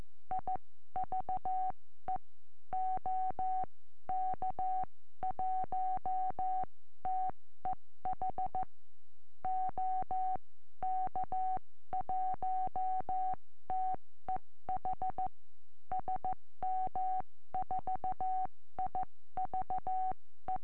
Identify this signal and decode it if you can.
Morse code, heard as a keyed tone, about 14 wpm, spaced as for 12 wpm, 750 Hz. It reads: IVE OK1TEH OK1TEH SM4IVE
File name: eme / sm4ive 70cm